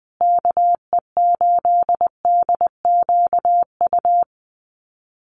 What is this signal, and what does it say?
Morse code, heard as a keyed tone, about 20 wpm, 700 Hz. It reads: KE8DQU
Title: USA Call Signs Repeated 3x 20wpm